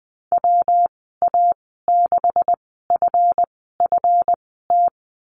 Morse code, heard as a keyed tone, about 20 wpm, spaced as for 14 wpm, 700 Hz. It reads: WA6FFT
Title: 154 Lesson 25g - Letter p - Call Signs 20wpm